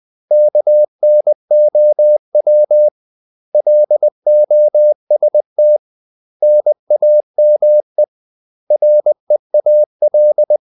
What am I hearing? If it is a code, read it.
Morse code, heard as a keyed tone, about 20 wpm, 600 Hz. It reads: KNOW LOST NAME REAL
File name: CW / Practice / 500MostCommonEnglishWords20